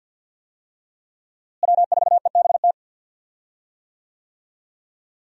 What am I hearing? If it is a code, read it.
Morse code, heard as a keyed tone, about 47 wpm, 700 Hz. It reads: W4EBT
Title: USA Call Signs - 5 Characters - Repeated 3x - 50wpm